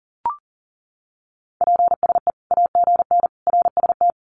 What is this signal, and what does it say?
Morse code, heard as a keyed tone, about 40 wpm, 700 Hz. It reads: PSE AGN RST
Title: Common QSO Phrases 40wpm